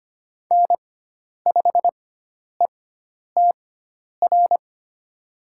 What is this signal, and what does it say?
Morse code, heard as a keyed tone, about 25 wpm, spaced as for 10 wpm, 700 Hz. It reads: N5ETR
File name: Lesson 16g - Letter c - Call Signs 25wpm